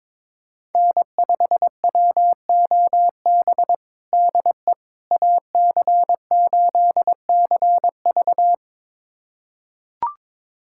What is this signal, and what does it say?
Morse code, heard as a keyed tone, about 22 wpm, 700 Hz. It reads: N5WOB DE AC8CV
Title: USA Callsign DE Callsign - 5 Characters - 22wpm